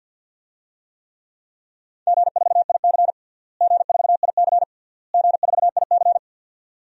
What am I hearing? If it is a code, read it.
Morse code, heard as a keyed tone, about 51 wpm, 700 Hz. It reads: G4IC G4IC G4IC
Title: DXCC-100 Call Signs - 4 Characters - Repeated 3x - 50wpm